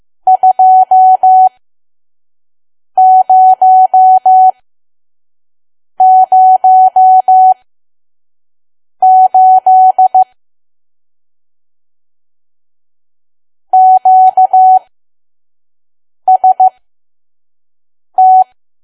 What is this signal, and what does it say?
Morse code, heard as a keyed tone, about 15 wpm, spaced as for 5 wpm, 750 Hz. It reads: 2008 QST
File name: Archive / 5 WPM files / PM